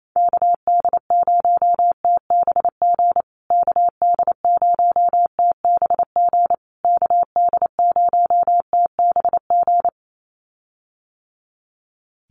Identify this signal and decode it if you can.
Morse code, heard as a keyed tone, about 28 wpm, 700 Hz. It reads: KD0TBG KD0TBG KD0TBG
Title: USA Call Signs Repeated 3x 28wpm